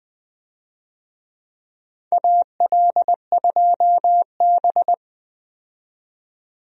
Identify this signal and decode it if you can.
Morse code, heard as a keyed tone, about 20 wpm, 700 Hz. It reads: AL2B
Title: USA Call Signs - 4 Characters - 20wpm